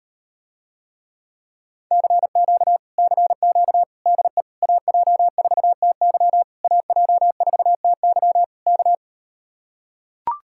Morse code, heard as a keyed tone, about 38 wpm, 700 Hz. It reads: CQ CQ DE AJ4TY AJ4TY K